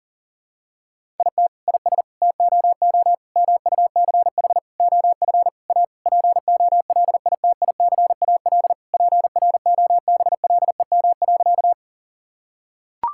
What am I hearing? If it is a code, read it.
Morse code, heard as a keyed tone, about 40 wpm, 700 Hz. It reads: IT IS TOO MUCH OF A POLITICAL PROBLEM.
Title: Sentences from Top 1000 Words - Review Part 5 - 40wpm